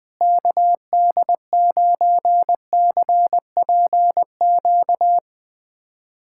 Morse code, heard as a keyed tone, about 20 wpm, 700 Hz. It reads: KD9CPQ
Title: USA Callsign DE Callsign 20wpm